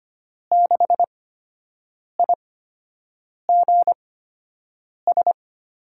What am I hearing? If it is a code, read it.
Morse code, heard as a keyed tone, about 25 wpm, spaced as for 7 wpm, 700 Hz. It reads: 6IGS
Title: USA Call Signs - 6 Characters - 8x Character Spacing - 25wpm